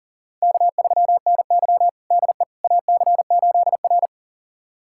Morse code, heard as a keyed tone, about 40 wpm, 700 Hz. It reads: K3NY DE AC8R
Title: USA Callsign DE Callsign - 4 Characters - 40wpm